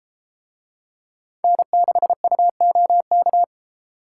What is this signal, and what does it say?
Morse code, heard as a keyed tone, about 32 wpm, 700 Hz. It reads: N6UOK